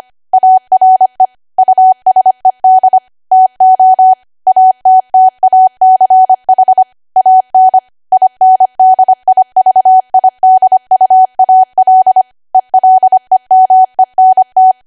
Morse code, heard as a keyed tone, about 25 wpm, 750 Hz. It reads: ARE USED TO ATTACH AN INDIVIDUAL ELEMENT
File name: cwa / W1AW-Level3 / 25WPM